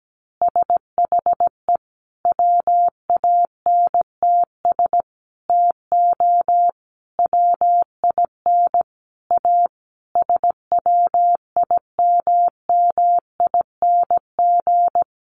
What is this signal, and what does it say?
Morse code, heard as a keyed tone, about 17 wpm, 700 Hz. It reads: SHE WANTS TO WIN A SWIMMING
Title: Additional Easy Phrases 17wpm